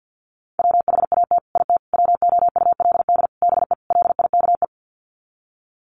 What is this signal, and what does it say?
Morse code, heard as a keyed tone, about 50 wpm, 700 Hz. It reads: WHAT IT WOULD BE LIKE
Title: Sets of 5 Words Encore 50wpm